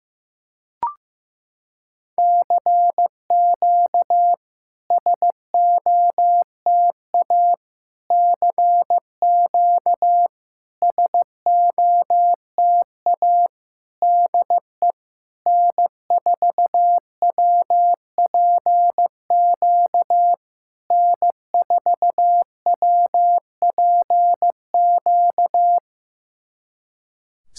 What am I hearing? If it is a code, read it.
Morse code, heard as a keyed tone, about 14 wpm, 700 Hz. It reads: CQ SOTA CQ SOTA DE N4WPQ N4WPQ
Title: SOTA QSO Practice 15wpm